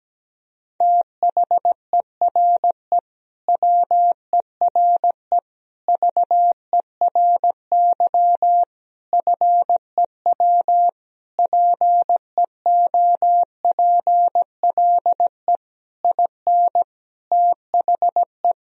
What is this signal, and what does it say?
Morse code, heard as a keyed tone, about 17 wpm, 700 Hz. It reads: THERE WERE VERY FEW PEOPLE IN THE
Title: Sentences from Top 1000 Words - Review Part 2 - 17wpm